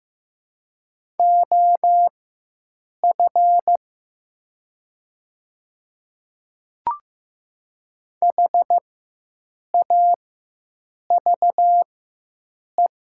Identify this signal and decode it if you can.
Morse code, heard as a keyed tone, about 15 wpm, spaced as for 7 wpm, 700 Hz. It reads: OF HAVE
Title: 4x Character Spacing - Top 100 Words - 15wpm